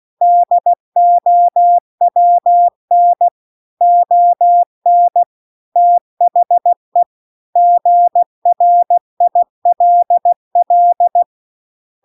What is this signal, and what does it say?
Morse code, heard as a keyed tone, about 16 wpm, 700 Hz. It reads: DOWN ON THE GRILL
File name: CW-The-Big-Twoheaded-River-2-16WPM